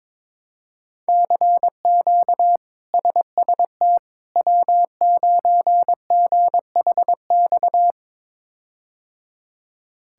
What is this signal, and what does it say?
Morse code, heard as a keyed tone, about 22 wpm, 700 Hz. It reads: CQ SST W9GHX
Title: K1USN SST QSO Practice - 22wpm